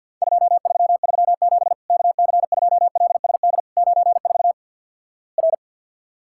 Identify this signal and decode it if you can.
Morse code, heard as a keyed tone, about 50 wpm, 700 Hz. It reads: 2337 KC2LSD 94 R
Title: Worked All Europe DX Contest - QTC Exchange with Repeats - 50wpm